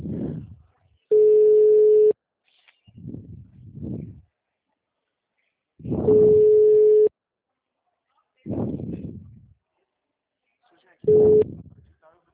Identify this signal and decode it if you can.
Ringback tone (Europe, 425 Hz), with speech.